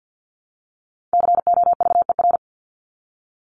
Morse code, heard as a keyed tone, about 50 wpm, 700 Hz. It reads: COVER
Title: Top 500 Words Repeated 50wpm